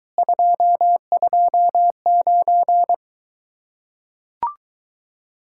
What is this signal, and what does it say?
Morse code, heard as a keyed tone, about 24 wpm, 700 Hz. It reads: 229